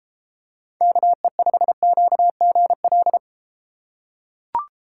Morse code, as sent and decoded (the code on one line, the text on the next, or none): -.- . ..... --.- --. .-..
KE5QGL